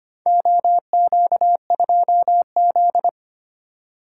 Morse code, heard as a keyed tone, about 25 wpm, 700 Hz. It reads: OQ2Z
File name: practice / 25wpm / Four Letter-Number Combinations - 25wpm